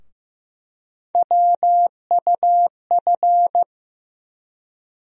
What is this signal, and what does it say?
Morse code, heard as a keyed tone, about 15 wpm, 700 Hz. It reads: WUF
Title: USA Call Sign Postfixes - 3 Letters 15wpm